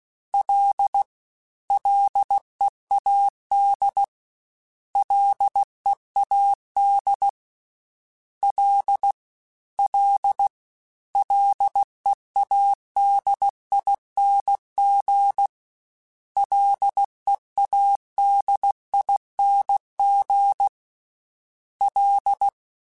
Morse code, heard as a keyed tone, about 16 wpm, 800 Hz. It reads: L LEAD LEAD L L LEADING LEADING L
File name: Ls / Single L m3s / Single L @15